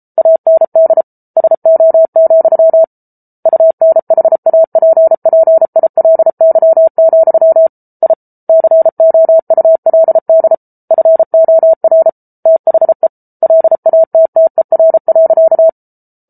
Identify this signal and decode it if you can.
Morse code, heard as a keyed tone, about 34 wpm, 650 Hz. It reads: AND SO, UNHAPPILY, I COULD FOR THE LATTER.